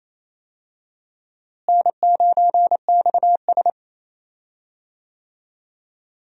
Morse code, heard as a keyed tone, about 27 wpm, 700 Hz. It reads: N9XS
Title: USA Call Signs - 4 Characters - Repeated 3x - 28wpm